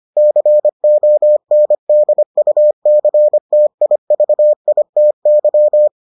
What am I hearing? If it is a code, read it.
Morse code, heard as a keyed tone, about 25 wpm, 600 Hz. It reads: CONDUCTIVITY